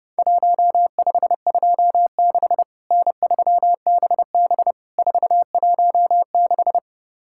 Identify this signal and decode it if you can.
Morse code, heard as a keyed tone, about 30 wpm, 700 Hz. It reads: 1526 N3BB 416
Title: Worked All Europe DX Contest - QTC Exchange - 30wpm